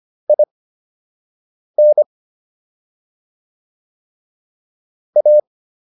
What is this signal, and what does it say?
Morse code, heard as a keyed tone, about 25 wpm, spaced as for 6 wpm, 600 Hz. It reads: IN A